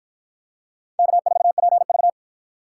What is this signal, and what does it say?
Morse code, heard as a keyed tone, about 50 wpm, 700 Hz. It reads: K4PV